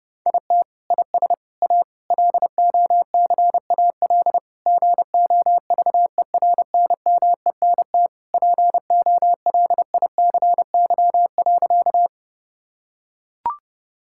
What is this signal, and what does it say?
Morse code, heard as a keyed tone, about 30 wpm, 700 Hz. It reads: IT IS A LOCAL GOVERNMENT POLICY.
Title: Sentences from Top 1000 Words - Review Part 5 - 30wpm